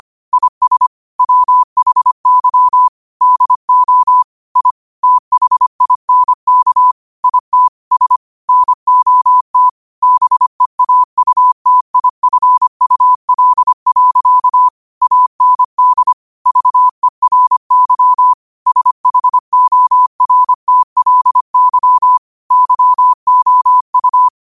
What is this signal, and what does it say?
Morse code, heard as a keyed tone, about 25 wpm, 1000 Hz. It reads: IS WHY DO I THINK IT S NOT BEAUTIFUL. AND VERY SHORTLY YOU
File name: qotdfiles / 25wpm